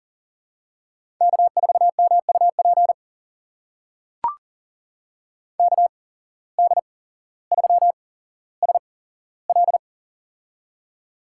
Morse code, heard as a keyed tone, about 42 wpm, spaced as for 17 wpm, 700 Hz. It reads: K4MUP KD3SL
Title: USA Call Signs - 5 Characters - 8x Character Spacing - 40wpm